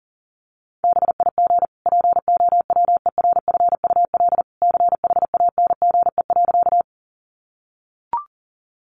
Morse code, heard as a keyed tone, about 40 wpm, 700 Hz. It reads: BIG POWERFUL CHANGE.
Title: Sentences from Top 1300 Words 40wpm